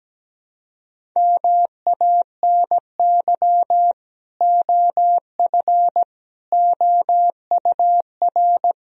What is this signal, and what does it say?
Morse code, heard as a keyed tone, about 17 wpm, 700 Hz. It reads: MANY OF OUR
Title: Sentences from Top 1000 Words - Review Part 3 - 17wpm